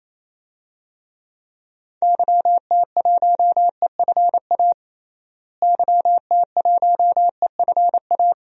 Morse code, heard as a keyed tone, about 28 wpm, 700 Hz. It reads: YT1EFA YT1EFA
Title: DXCC-100 Call Signs - 6 Characters - Repeated 2x - 28wpm